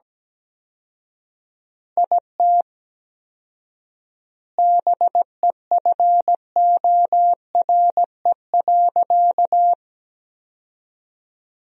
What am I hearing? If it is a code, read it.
Morse code, heard as a keyed tone, about 17 wpm, 700 Hz. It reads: IT BEFORE.
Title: 4x Word-Spacing - Sentences from Top 400 Words - 17wpm